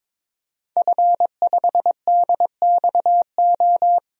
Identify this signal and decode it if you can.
Morse code, heard as a keyed tone, about 22 wpm, 700 Hz. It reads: F5DXO
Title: DXCC-100 Call Signs - 5 Characters - Repeated 3x - 22wpm